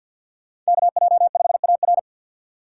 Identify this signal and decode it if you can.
Morse code, heard as a keyed tone, about 47 wpm, 700 Hz. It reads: KJ5AR
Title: USA Call Signs - 5 Characters - 4x Character Spacing - 50wpm